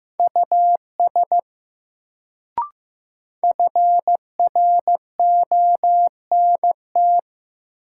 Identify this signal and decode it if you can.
Morse code, heard as a keyed tone, about 15 wpm, 700 Hz. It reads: US FRONT
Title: Sets of 2 Words - New Word Drill - 15wpm